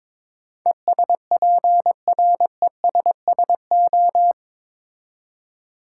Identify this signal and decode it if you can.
Morse code, heard as a keyed tone, about 22 wpm, 700 Hz. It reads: ESPRESSO
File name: Difficult Words with Hints - 22wpm